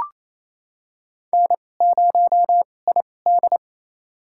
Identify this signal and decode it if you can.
Morse code, heard as a keyed tone, about 28 wpm, spaced as for 20 wpm, 700 Hz. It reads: N0ID
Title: USA Call Signs - 2x Character Spacing - 28wpm